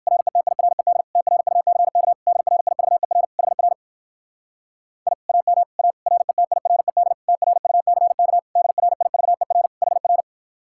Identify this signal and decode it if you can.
Morse code, heard as a keyed tone, about 59 wpm, 700 Hz. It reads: RETIRED TRUCK DRIVER HR I AM A RETIRED TRUCK DRIVER HR